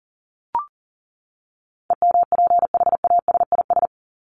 Morse code, heard as a keyed tone, about 40 wpm, 700 Hz. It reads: EMPHASIS
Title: Top 1251-1500 Words 40wpm